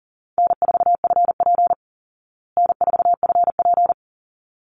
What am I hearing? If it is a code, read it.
Morse code, heard as a keyed tone, about 40 wpm, 700 Hz. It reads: N4FP N4FP
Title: Easy Call Signs Repeated 2x - 40wpm